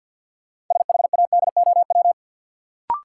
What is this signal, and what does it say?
Morse code, heard as a keyed tone, about 47 wpm, 700 Hz. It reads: SHADOW
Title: Sets of 5 Words Encore - New Word Drill - 50wpm